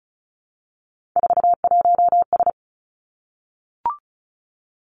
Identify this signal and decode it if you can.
Morse code, heard as a keyed tone, about 36 wpm, 700 Hz. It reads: 41S
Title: Three Challenging Letter-Number Combinations 35wpm